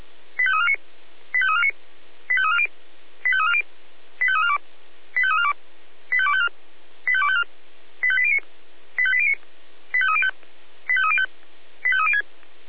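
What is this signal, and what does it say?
Five-tone selective call (ZVEI), heard as quick runs of five tones, tones of about 70 ms. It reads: 85349 85349 85340 85340 85342 85342 85254 85254 85809 85809 85305 85305 85306
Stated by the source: Five-tone ZVEI selective call received by SDR, from the audio server of Feuerwehr Orlamünde (fire brigade).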